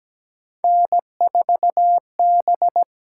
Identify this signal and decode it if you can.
Morse code, heard as a keyed tone, about 17 wpm, 700 Hz. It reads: N4B
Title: USA Call Signs - 5 Characters - 8x Character Spacing - 17wpm